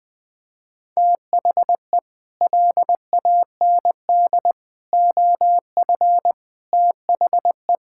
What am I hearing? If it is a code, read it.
Morse code, heard as a keyed tone, about 20 wpm, 700 Hz. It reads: THE LAND OF THE
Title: Sets of 9 Words - 20wpm